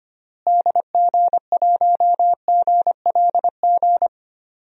Morse code, heard as a keyed tone, about 25 wpm, 700 Hz. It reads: DG1GLG
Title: DXCC-100 Call Signs - 6 Characters - Repeated 2x - 25wpm